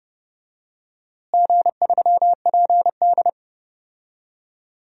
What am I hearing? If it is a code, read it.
Morse code, heard as a keyed tone, about 32 wpm, 700 Hz. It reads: G3PD
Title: DXCC-100 Call Signs - 4 Characters - Repeated 2x - 30wpm